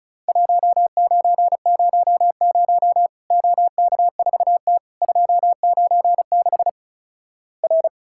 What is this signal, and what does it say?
Morse code, heard as a keyed tone, about 36 wpm, 700 Hz. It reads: 1900 OK4T 296 R
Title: Worked All Europe DX Contest - QTC Exchange - 35wpm